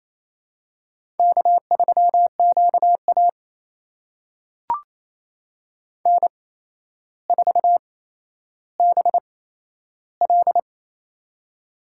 Morse code, heard as a keyed tone, about 29 wpm, spaced as for 12 wpm, 700 Hz. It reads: K3QA N4BL
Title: USA Call Signs - 4 Characters - 8x Character Spacing - 28wpm